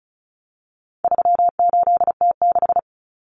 Morse code, heard as a keyed tone, about 34 wpm, 700 Hz. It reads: 38T6
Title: Common Cut Numbers - 4-Digit 35wpm